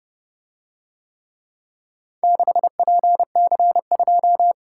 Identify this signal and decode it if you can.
Morse code, heard as a keyed tone, about 31 wpm, 700 Hz. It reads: 6PC2